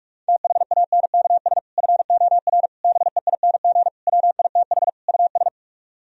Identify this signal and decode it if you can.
Morse code, heard as a keyed tone, about 40 wpm, 700 Hz. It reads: THANKS FOR BEING WITH US